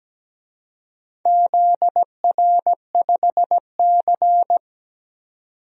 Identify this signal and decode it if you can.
Morse code, heard as a keyed tone, about 17 wpm, 700 Hz. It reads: ZR5C
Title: DXCC-100 Call Signs - 4 Characters - 17wpm